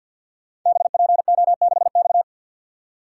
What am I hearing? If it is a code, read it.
Morse code, heard as a keyed tone, about 50 wpm, 700 Hz. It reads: BPO6X
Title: Code Groups - 5 Letter:Numbers - 50wpm